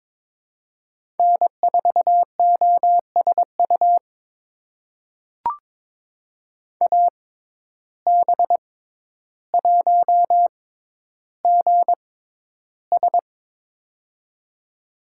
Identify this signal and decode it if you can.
Morse code, heard as a keyed tone, about 21 wpm, spaced as for 11 wpm, 700 Hz. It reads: N4OSU AB1GS
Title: USA Call Signs - 5 Characters - 6x Character Spacing - 22wpm